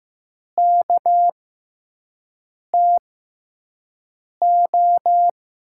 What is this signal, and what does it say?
Morse code, heard as a keyed tone, about 15 wpm, spaced as for 5 wpm, 700 Hz. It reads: KTO
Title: USA Call Signs - 5 Characters - 6x Character Spacing - 15wpm